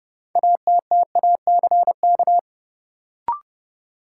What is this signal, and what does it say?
Morse code, heard as a keyed tone, about 30 wpm, 700 Hz. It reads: ATTACK
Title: Sentences from Top 1500 Words - New Word Drill Alt - 30wpm